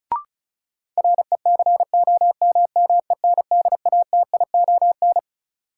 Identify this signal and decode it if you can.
Morse code, heard as a keyed tone, about 36 wpm, 700 Hz. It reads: RECOMMENDATION
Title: Sentences from Top 1400 Words - New Word Drill - 35wpm